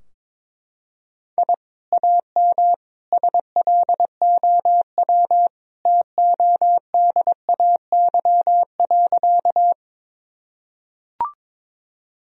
Morse code, heard as a keyed tone, about 22 wpm, 700 Hz. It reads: I AM SLOW TODAY.
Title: Sentences from Top 1000 Words - Review Part 2 - 22wpm